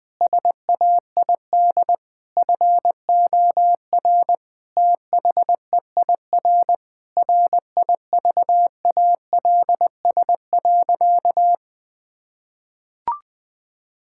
Morse code, heard as a keyed tone, about 20 wpm, 700 Hz. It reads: SAID FOR THEIR RIVALS.
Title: Sentences with 7-Word Phrases - Spoken Beginning - 20wpm